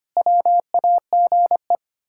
Morse code, heard as a keyed tone, about 25 wpm, 700 Hz. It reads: WAGE
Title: Pangram Challenge v2 - 25wpm